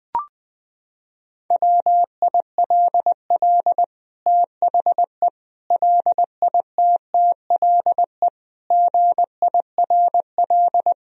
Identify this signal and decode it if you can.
Morse code, heard as a keyed tone, about 20 wpm, 700 Hz. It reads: WILL THE LITTLE GIRL
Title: Sentences from Top 1000 Words - Review Part 1 - 20wpm